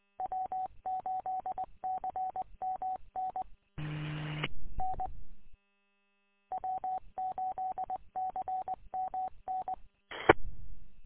Morse code, heard as a keyed tone, about 20 wpm, 750 Hz. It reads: W8CMN N W8CMN E
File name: w8cmn- -combined